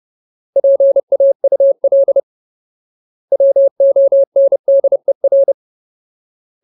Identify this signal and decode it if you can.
Morse code, heard as a keyed tone, about 30 wpm, 550 Hz. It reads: PAUL WONDER
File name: Ultimate Word List / 30WPM Ultimate